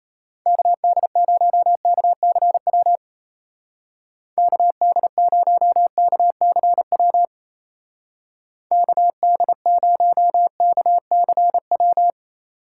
Morse code, heard as a keyed tone, about 34 wpm, 700 Hz. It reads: KD0KCW KD0KCW KD0KCW
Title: Speed Racing - USA 2x3 Call Signs 28wpm